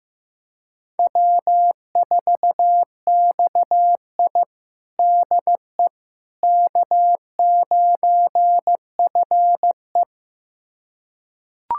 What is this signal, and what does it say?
Morse code, heard as a keyed tone, about 14 wpm, 700 Hz. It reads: W4XI DE K9FE
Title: USA Callsign DE Callsign - 4 Characters - 15wpm